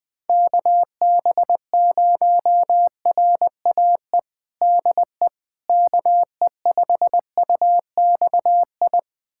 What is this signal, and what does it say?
Morse code, heard as a keyed tone, about 20 wpm, 700 Hz. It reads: KB0RAE DE KE5UXI
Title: USA Callsign DE Callsign 20wpm